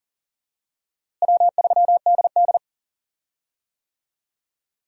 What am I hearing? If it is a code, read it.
Morse code, heard as a keyed tone, about 37 wpm, 700 Hz. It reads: W3DD
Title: USA Call Signs - 4 Characters - Repeated 2x - 40wpm